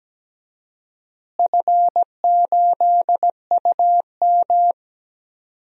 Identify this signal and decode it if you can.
Morse code, heard as a keyed tone, about 17 wpm, 700 Hz. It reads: F8UM
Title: DXCC-100 Call Signs - 4 Characters - Repeated 2x - 17wpm